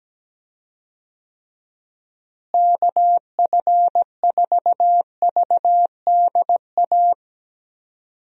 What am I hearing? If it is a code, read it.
Morse code, heard as a keyed tone, about 17 wpm, 700 Hz. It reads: KF4VDA
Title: USA Call Signs - 6 Characters - Repeated 2x - 17wpm